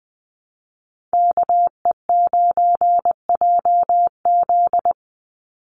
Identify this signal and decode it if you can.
Morse code, heard as a keyed tone, about 20 wpm, 700 Hz. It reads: KE9JZ
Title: USA Call Signs 20wpm